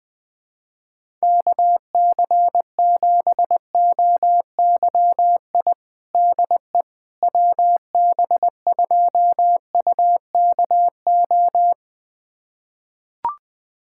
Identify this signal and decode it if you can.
Morse code, heard as a keyed tone, about 20 wpm, 700 Hz. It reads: KC7OYI DE WB2UKO
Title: USA Callsign DE Callsign - 6 Characters - 20wpm